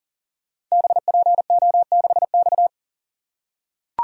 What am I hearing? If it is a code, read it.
Morse code, heard as a keyed tone, about 42 wpm, 700 Hz. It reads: BPO6X
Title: Code Groups - 5 Letter:Numbers - 40wpm